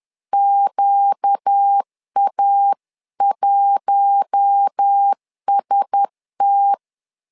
Morse code, heard as a keyed tone, about 10 wpm, 800 Hz. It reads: QA1ST